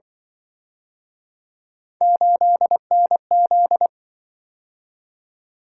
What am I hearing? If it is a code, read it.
Morse code, heard as a keyed tone, about 25 wpm, 700 Hz. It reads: 8NZ